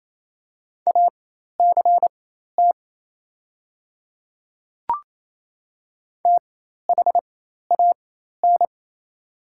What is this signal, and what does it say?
Morse code, heard as a keyed tone, about 27 wpm, spaced as for 13 wpm, 690 Hz. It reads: ACT THAN
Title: 4x Character Spacing - Top 200 Words - 28wpm